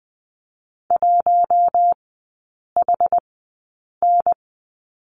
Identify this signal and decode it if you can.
Morse code, heard as a keyed tone, about 20 wpm, spaced as for 8 wpm, 700 Hz. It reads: 1HN
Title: 074 Lesson 13e - Letter L - Three Characters 20wpm